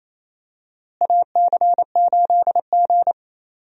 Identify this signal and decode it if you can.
Morse code, heard as a keyed tone, about 28 wpm, 700 Hz. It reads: AC8G
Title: USA Call Signs - 4 Characters - Repeated 2x - 28wpm